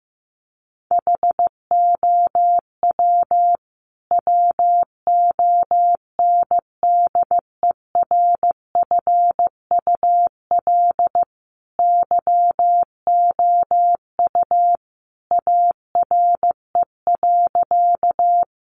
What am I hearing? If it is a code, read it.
Morse code, heard as a keyed tone, about 15 wpm, 700 Hz. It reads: HOW WONDERFUL YOU ARE.
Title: Sentences from Top 1100 Words 15wpm